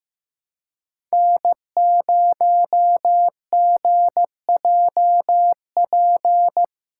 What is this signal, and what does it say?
Morse code, heard as a keyed tone, about 15 wpm, 700 Hz. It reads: N0GJP
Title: USA Call Signs - 5 Characters - 4x Character Spacing - 15wpm